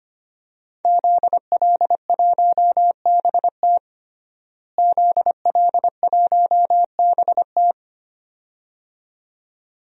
Morse code, heard as a keyed tone, about 25 wpm, 700 Hz. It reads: ZL1BT ZL1BT
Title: DXCC-100 Call Signs - 5 Characters - Repeated 2x - 25wpm